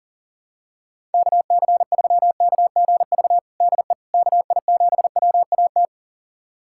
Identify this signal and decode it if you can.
Morse code, heard as a keyed tone, about 40 wpm, 700 Hz. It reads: KC3KGV DE KI7WAT
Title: USA Callsign DE Callsign 40wpm